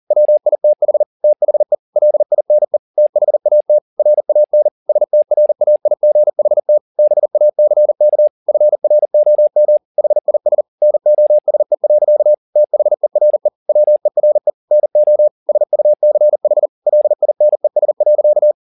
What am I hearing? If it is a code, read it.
Morse code, heard as a keyed tone, about 40 wpm, 600 Hz. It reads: WITH THE LINE THAT RAN STRAIGHT BACK FROM HIS NOSE. THERE WERE NO SUCH LINES.
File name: the old man and the sea 2 40wpm